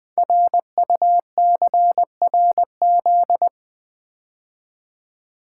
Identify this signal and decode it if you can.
Morse code, heard as a keyed tone, about 20 wpm, 700 Hz. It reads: RUCRZ
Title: Code Groups - 5 Letters - 20wpm